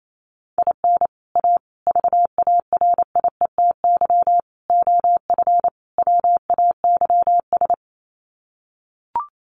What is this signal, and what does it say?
Morse code, heard as a keyed tone, about 28 wpm, 700 Hz. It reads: IN A VARIETY OF WAYS
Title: Sets of 5 Words Encore 28wpm